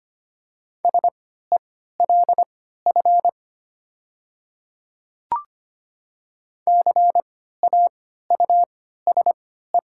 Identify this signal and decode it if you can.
Morse code, heard as a keyed tone, about 25 wpm, spaced as for 14 wpm, 700 Hz. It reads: SELF CAUSE
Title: 3x Character Spacing - Top 300 Words - 25wpm